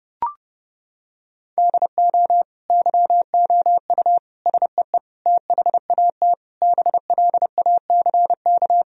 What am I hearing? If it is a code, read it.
Morse code, heard as a keyed tone, about 30 wpm, 700 Hz. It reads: DO YOU SEE THAT BLACK
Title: Sentences from Top 1000 Words - Review Part 2 - 30wpm